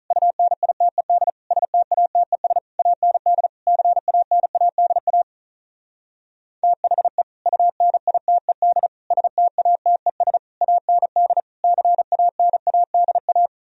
Morse code, heard as a keyed tone, about 43 wpm, 700 Hz. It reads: UNITED STATES AND CANADA THE UNITED STATES AND CANADA
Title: Speed-Racing - Sets of 5 Words 35wpm